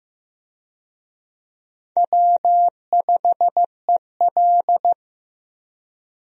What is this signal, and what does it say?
Morse code, heard as a keyed tone, about 15 wpm, 700 Hz. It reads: W5EL
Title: USA Call Signs - 4 Characters - Repeated 3x - 15wpm